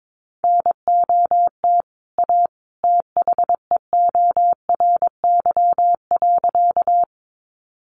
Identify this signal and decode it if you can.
Morse code, heard as a keyed tone, about 22 wpm, 700 Hz. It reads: NOT A THEORY.
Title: Sentences from Top 600 Words 22wpm